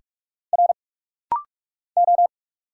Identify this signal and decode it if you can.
Morse code, heard as a keyed tone, about 46 wpm, 700 Hz. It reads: R O
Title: Single Letters - Rapid-Fire 45wpm